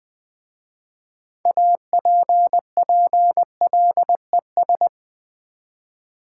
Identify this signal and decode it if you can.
Morse code, heard as a keyed tone, about 20 wpm, 700 Hz. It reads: APPLES
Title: Easy Words with Hints - 20wpm